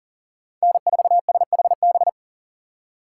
Morse code, heard as a keyed tone, about 40 wpm, 700 Hz. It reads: N4SHB
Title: USA Call Signs - 4x Character Spacing - 40wpm